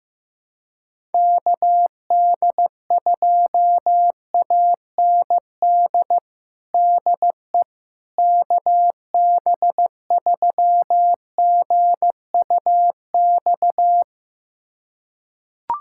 Morse code, heard as a keyed tone, about 15 wpm, 700 Hz. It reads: KD2AND DE KB3GUX
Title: USA Callsign DE Callsign - 6 Characters - 15wpm